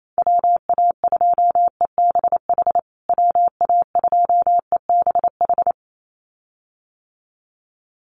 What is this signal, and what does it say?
Morse code, heard as a keyed tone, about 28 wpm, 700 Hz. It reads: WA2EBH WA2EBH
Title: USA Call Signs Repeated 2x 28wpm